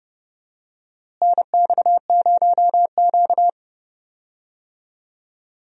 Morse code, heard as a keyed tone, about 30 wpm, 700 Hz. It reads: NX0Q
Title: USA Call Signs - 4 Characters - Repeated 3x - 30wpm